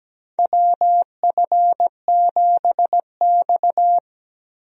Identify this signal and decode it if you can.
Morse code, heard as a keyed tone, about 17 wpm, 700 Hz. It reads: WF7X
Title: USA Call Signs - 4 Characters - 17wpm